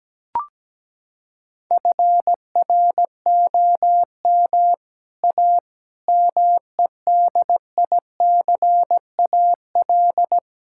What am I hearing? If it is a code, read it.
Morse code, heard as a keyed tone, about 17 wpm, 700 Hz. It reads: FROM A MEDICAL
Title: Sentences from Top 1000 Words - Review Part 5 - 17wpm